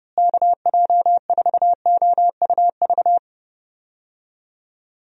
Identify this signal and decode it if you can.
Morse code, heard as a keyed tone, about 30 wpm, 700 Hz. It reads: KJ4OUV
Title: USA Call Signs - 6 Characters - Repeated 3x - 30wpm